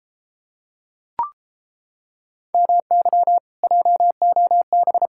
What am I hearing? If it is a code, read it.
Morse code, heard as a keyed tone, about 33 wpm, 700 Hz. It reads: MY JOB